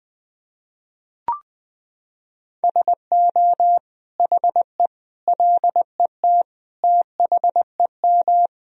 Morse code, heard as a keyed tone, about 20 wpm, 700 Hz. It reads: SO HE LET THEM
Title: Sentences from Top 1000 Words - Review Part 2 - 20wpm